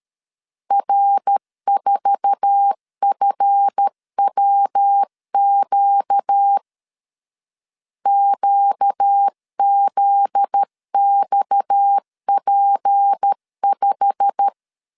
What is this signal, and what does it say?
Morse code, heard as a keyed tone, about 13 wpm, 800 Hz. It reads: R4FWQ QZXP5
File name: LEZIONI / 9 28 12wpm 2351